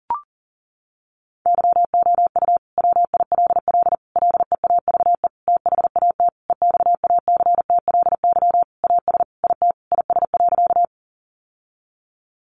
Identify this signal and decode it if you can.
Morse code, heard as a keyed tone, about 40 wpm, 700 Hz. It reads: YOU WILL LEAVE THAT EXACTLY AS IT IS.
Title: Sentences from Top 900 Words 40wpm